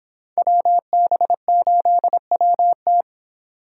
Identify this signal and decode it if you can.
Morse code, heard as a keyed tone, about 23 wpm, 700 Hz. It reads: WB8WT